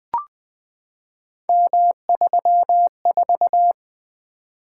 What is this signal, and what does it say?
Morse code, heard as a keyed tone, about 20 wpm, 700 Hz. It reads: M34
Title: Three Letter-Number Combinations - 20wpm